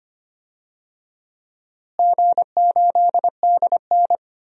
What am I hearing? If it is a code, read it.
Morse code, heard as a keyed tone, about 25 wpm, 700 Hz. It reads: G8DN
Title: DXCC-100 Call Signs - 4 Characters - Repeated 3x - 25wpm